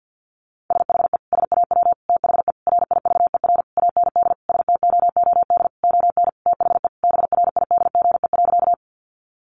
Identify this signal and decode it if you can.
Morse code, heard as a keyed tone, about 50 wpm, 700 Hz. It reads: SHE SAW THE RIVER AND STOOD ON THE BRIDGE.
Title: Sentences from Top 1200 Words 50wpm